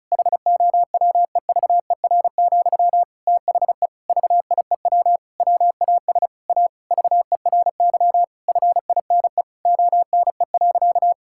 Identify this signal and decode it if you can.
Morse code, heard as a keyed tone, about 36 wpm, 700 Hz. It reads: HOWEVER, THE VIEW WAS A VERY FINE ONE.
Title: Sentences from Top 1000 Words - Review Part 6 - 35wpm